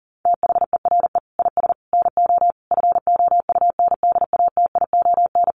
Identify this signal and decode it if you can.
Morse code, heard as a keyed tone, about 40 wpm, 700 Hz. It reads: THERE IS NO FOUNDATION
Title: Sentences from Top 1200 Words 40wpm